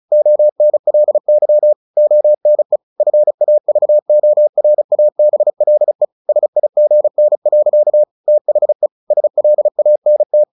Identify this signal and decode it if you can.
Morse code, heard as a keyed tone, about 34 wpm, 600 Hz. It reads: ONLY ONE FAVORABLE SIGN. THE SLANT
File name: the old man and the sea / the old man and the sea 1 35wpm